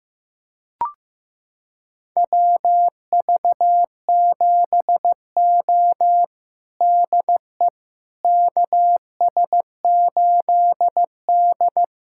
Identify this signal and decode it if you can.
Morse code, heard as a keyed tone, about 15 wpm, 700 Hz. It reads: WV7O DE KS8D
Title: USA Callsign DE Callsign - 4 Characters - 15wpm